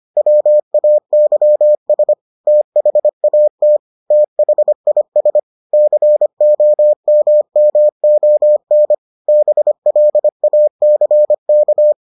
Morse code, heard as a keyed tone, about 25 wpm, 600 Hz. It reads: WAYS THAT THIS COMMON BLACK